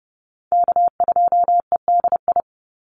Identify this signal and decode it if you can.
Morse code, heard as a keyed tone, about 30 wpm, 700 Hz. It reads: K2EDI
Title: USA Call Signs 30wpm